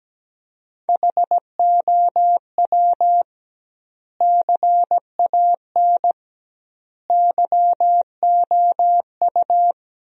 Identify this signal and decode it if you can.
Morse code, heard as a keyed tone, about 17 wpm, 700 Hz. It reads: HOW CAN YOU
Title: 2x Word-Spacing - Sentences from Top 200 Words - 17wpm